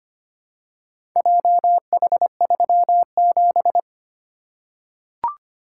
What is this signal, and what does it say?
Morse code, heard as a keyed tone, about 25 wpm, 700 Hz. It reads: JH37
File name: practice / 25wpm / Four Letter-Number Combinations - 25wpm